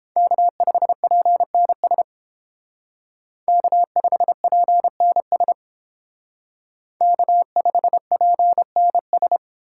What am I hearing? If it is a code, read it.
Morse code, heard as a keyed tone, about 30 wpm, 700 Hz. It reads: K5PNS K5PNS K5PNS